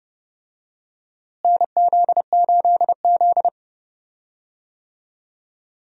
Morse code, heard as a keyed tone, about 30 wpm, 700 Hz. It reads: NZ8Z
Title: USA Call Signs - 4 Characters - Repeated 2x - 30wpm